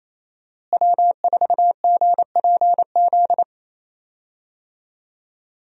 Morse code, heard as a keyed tone, about 28 wpm, 700 Hz. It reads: W4GPZ
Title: USA Call Signs - 5 Characters - Repeated 3x - 28wpm